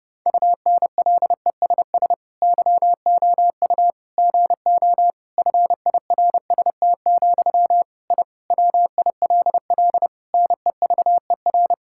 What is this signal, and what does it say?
Morse code, heard as a keyed tone, about 30 wpm, 700 Hz. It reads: UNLESS YOU GO FIRST, I WILL NEVER
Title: Sentences from Top 1000 Words - Review Part 6 - 30wpm